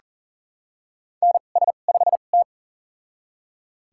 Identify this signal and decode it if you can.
Morse code, heard as a keyed tone, about 39 wpm, spaced as for 29 wpm, 700 Hz. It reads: NS5T
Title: USA Call Signs - 2x Character Spacing - 40wpm